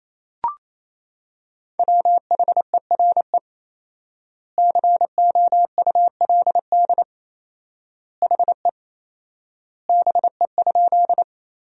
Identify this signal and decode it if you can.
Morse code, heard as a keyed tone, about 28 wpm, 700 Hz. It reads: WHERE COULD HE BE?
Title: 4x Word-Spacing - Sentences from Top 200 Words - 28wpm